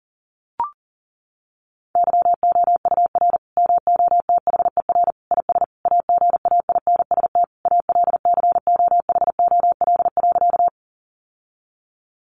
Morse code, heard as a keyed tone, about 40 wpm, 700 Hz. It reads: YOUR MOTHER IS AGAINST ALCOHOL.
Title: Sentences from Top 1400 Words 40wpm